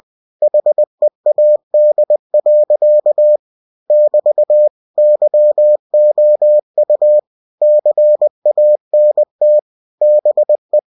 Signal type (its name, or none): Morse code